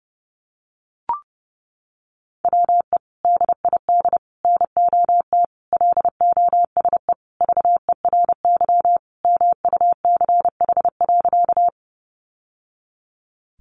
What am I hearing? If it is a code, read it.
Morse code, heard as a keyed tone, about 30 wpm, 700 Hz. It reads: WE DID NOT LOSE VERY MUCH.
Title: Sentences from Top 1100 Words 30wpm